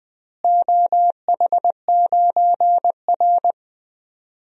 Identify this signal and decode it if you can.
Morse code, heard as a keyed tone, about 21 wpm, 700 Hz. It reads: OH9R
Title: DXCC-100 Call Signs - 4 Characters - Repeated 3x - 20wpm